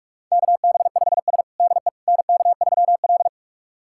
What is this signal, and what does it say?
Morse code, heard as a keyed tone, about 42 wpm, 700 Hz. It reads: KB5S DE NK3L
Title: USA Callsign DE Callsign - 4 Characters - 45wpm